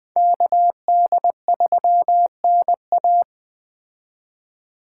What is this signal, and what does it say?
Morse code, heard as a keyed tone, about 21 wpm, 700 Hz. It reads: KD3NA